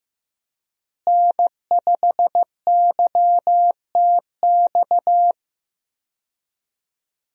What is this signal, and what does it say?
Morse code, heard as a keyed tone, about 15 wpm, 700 Hz. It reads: N5YTX
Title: USA Call Signs - 5 Characters - Repeated 3x - 15wpm